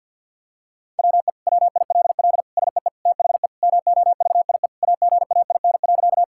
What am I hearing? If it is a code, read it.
Morse code, heard as a keyed tone, about 50 wpm, 700 Hz. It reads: WE WILL SEE THE MOVIE AGAIN.
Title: Sentences from Top 1000 Words - Review Part 5 - 50wpm